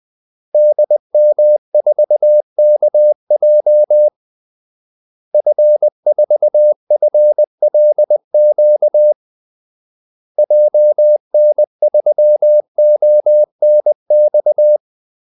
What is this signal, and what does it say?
Morse code, heard as a keyed tone, about 20 wpm, 600 Hz. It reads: DM4KJ F4FLQ JN3ONX
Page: Kaldesignaler 20 wpm